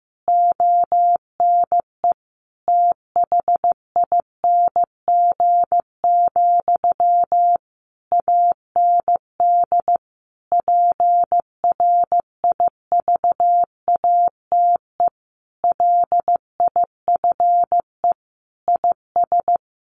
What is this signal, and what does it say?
Morse code, heard as a keyed tone, about 15 wpm, 700 Hz. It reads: ONE THING, AND PRIVATE LIFE IS
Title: Sentences from Top 1000 Words 15wpm